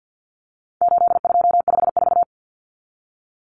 Morse code, heard as a keyed tone, about 50 wpm, 700 Hz. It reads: 8254
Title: Common Cut Numbers - 4-Digit 50wpm